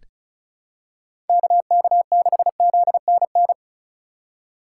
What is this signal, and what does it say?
Morse code, heard as a keyed tone, about 36 wpm, 700 Hz. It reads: KK6ZNN
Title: USA Call Signs - 3x Character Spacing - 35wpm